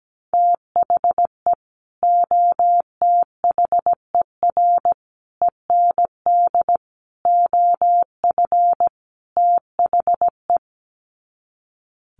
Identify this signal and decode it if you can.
Morse code, heard as a keyed tone, about 17 wpm, 700 Hz. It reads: THE OTHER END OF THE
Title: Sets of 5 Words 17wpm